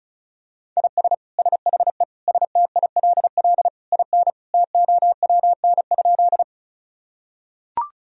Morse code, heard as a keyed tone, about 34 wpm, 700 Hz. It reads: IS SHE STILL IN TOWN?
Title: Sentences from Top 1000 Words - Review Part 2 - 35wpm